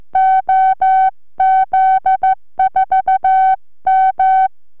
Morse code, heard as a keyed tone, about 14 wpm, 760 Hz. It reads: OZ4M
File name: multimedialne / 13cm QSO z OZ4MM ARRL2009